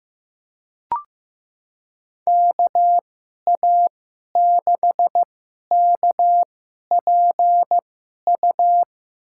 Morse code, heard as a keyed tone, about 15 wpm, spaced as for 11 wpm, 700 Hz. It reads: KA6KPU
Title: USA Call Signs - 6 Characters - 2x Character Spacing - 15wpm